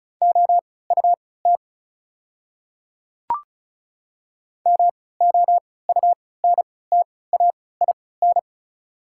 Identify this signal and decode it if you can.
Morse code, heard as a keyed tone, about 36 wpm, spaced as for 20 wpm, 700 Hz. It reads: OUT MOUNTAIN
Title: 3x Character Spacing - Top 500 Words - 35wpm